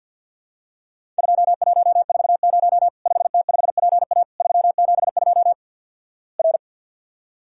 Morse code, heard as a keyed tone, about 52 wpm, 700 Hz. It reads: 2140 5T5PA 372 R
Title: Worked All Europe DX Contest - QTC Exchange with Repeats - 50wpm